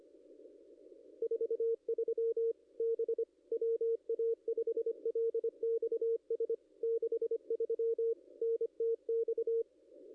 Morse code, heard as a keyed tone, about 25 wpm, 460 Hz. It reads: 43 B WA5LXS 63 NTX
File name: contests / ss-cw-k5tr-op-n5rz -0200